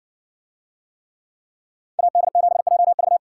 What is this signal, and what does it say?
Morse code, heard as a keyed tone, about 60 wpm, 700 Hz. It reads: AD7JV